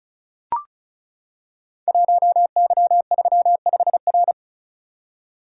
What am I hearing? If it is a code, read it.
Morse code, heard as a keyed tone, about 36 wpm, 690 Hz. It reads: 1Y35R
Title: Code Groups - 5 Letter:Numbers - 35wpm